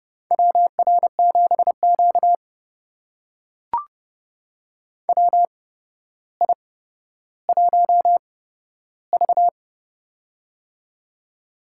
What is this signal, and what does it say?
Morse code, heard as a keyed tone, about 30 wpm, spaced as for 13 wpm, 700 Hz. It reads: WR7Q WI1V
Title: USA Call Signs - 4 Characters - 8x Character Spacing - 30wpm